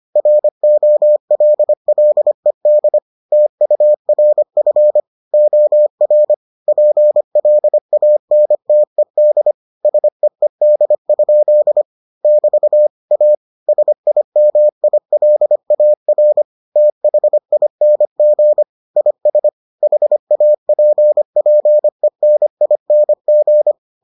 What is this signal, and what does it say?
Morse code, heard as a keyed tone, about 25 wpm, 600 Hz. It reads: ROLLED TURF OR PLANTED SEED? <BT> A SIMILAR THING IS HAPPENING